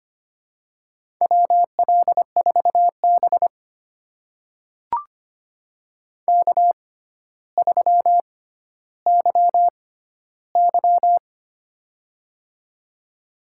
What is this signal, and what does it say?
Morse code, heard as a keyed tone, about 25 wpm, spaced as for 13 wpm, 700 Hz. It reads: WL4B K3YY
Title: USA Call Signs - 4 Characters - 6x Character Spacing - 25wpm